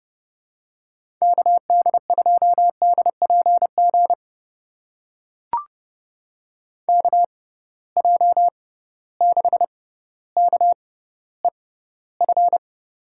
Morse code, heard as a keyed tone, about 31 wpm, spaced as for 15 wpm, 700 Hz. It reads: KD2DPG KJ6KEF
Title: USA Call Signs - 6 Characters - 6x Character Spacing - 30wpm